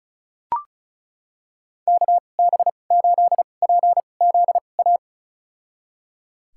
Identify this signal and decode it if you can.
Morse code, heard as a keyed tone, about 34 wpm, spaced as for 25 wpm, 700 Hz. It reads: KB8PZA
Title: USA Call Signs - 2x Character Spacing - 35wpm